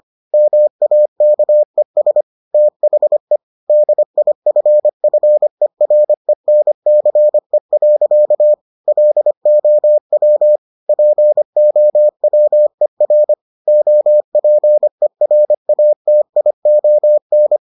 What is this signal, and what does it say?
Morse code, heard as a keyed tone, about 25 wpm, 600 Hz. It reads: MAKES THE DIFFERENCE. LOW POWER OPERATION